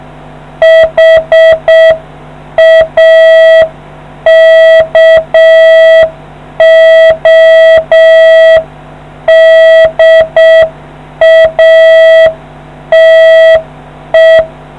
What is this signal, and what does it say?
Morse code, heard as a keyed tone, about 6 wpm, 650 Hz. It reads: HAKODATE